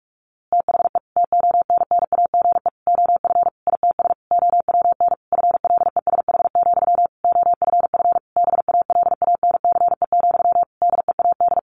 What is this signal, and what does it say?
Morse code, heard as a keyed tone, about 44 wpm, 700 Hz. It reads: THE TONNAGE OF ITS OWN FLESH, OFF BALANCE, DEAD
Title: A Sound Of Thunder 45wpm